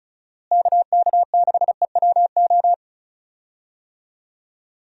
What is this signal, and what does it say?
Morse code, heard as a keyed tone, about 34 wpm, 700 Hz. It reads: KK6EWO
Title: USA Call Signs - 6 Characters - Repeated 2x - 35wpm